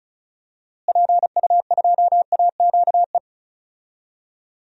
Morse code, heard as a keyed tone, about 36 wpm, 700 Hz. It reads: PU2AQE
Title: DXCC-100 Call Signs - 6 Characters - Repeated 2x - 35wpm